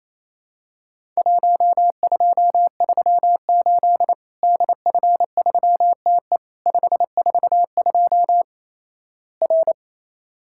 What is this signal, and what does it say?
Morse code, heard as a keyed tone, about 28 wpm, 700 Hz. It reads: 1238 DF3TE 542 R
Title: Worked All Europe DX Contest - QTC Exchange - 28wpm